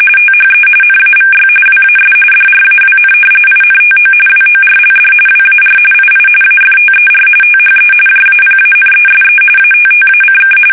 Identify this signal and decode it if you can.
RTTY, about 45.45 baud. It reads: THE NAVY WAS DRAWN INTO WORLD WAR II AFTER THE JAPANESE ATTACK
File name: fsk850 / usnavy1